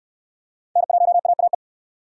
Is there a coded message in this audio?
Yes: Morse code from 0.8 to 1.5 s, heard as a keyed tone, about 63 wpm, 690 Hz. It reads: N1AAE